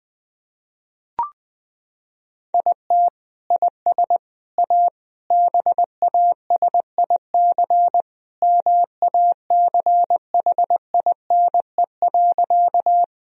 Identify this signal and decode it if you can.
Morse code, heard as a keyed tone, about 20 wpm, 700 Hz. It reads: IT IS A BASIC MACHINE.
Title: Sentences from Top 1000 Words - Review Part 3 - 20wpm